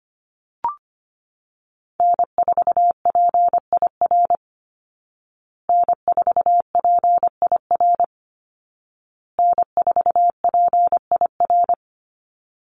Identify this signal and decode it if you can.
Morse code, heard as a keyed tone, about 25 wpm, 700 Hz. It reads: N4PIR N4PIR N4PIR
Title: Call Signs Repeated 2x - 25wpm